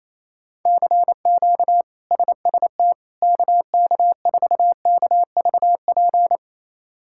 Morse code, heard as a keyed tone, about 28 wpm, 700 Hz. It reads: CQ SST KK4KVP
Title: K1USN SST QSO Practice - 28wpm